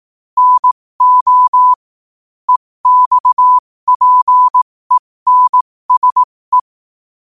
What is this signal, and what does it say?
Morse code, heard as a keyed tone, about 18 wpm, spaced as for 15 wpm, 1000 Hz. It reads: NO EXPENSE
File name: qotdfiles / 15wpm